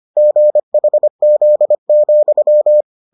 Morse code, heard as a keyed tone, about 25 wpm, 600 Hz. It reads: GHZ,